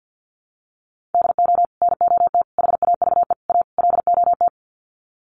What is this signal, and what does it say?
Morse code, heard as a keyed tone, about 50 wpm, 700 Hz. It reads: DO NOT HAVE A LOT
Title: Sets of 5 Words Encore 50wpm